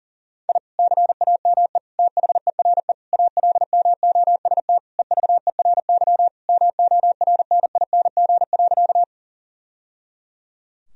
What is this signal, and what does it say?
Morse code, heard as a keyed tone, about 40 wpm, 700 Hz. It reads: I CAME THERE ALMOST EVERY MORNING.
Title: Sentences from Top 1000 Words - Review Part 4 - 40wpm